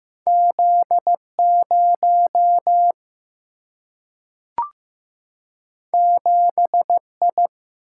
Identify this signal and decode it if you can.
Morse code, heard as a keyed tone, about 15 wpm, 700 Hz. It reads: Z 0 7 I
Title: Single-Letter Single-Number 15wpm